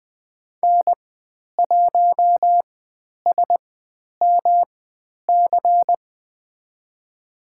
Morse code, heard as a keyed tone, about 20 wpm, spaced as for 10 wpm, 700 Hz. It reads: N1SMC
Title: 109 Lesson 18g - Letter m - Call Signs 20wpm